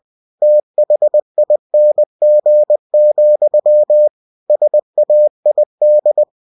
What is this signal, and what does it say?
Morse code, heard as a keyed tone, about 20 wpm, 600 Hz. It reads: THING, SAID
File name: cw-books / winnie-the-pooh / winnie-20wpm-page-0084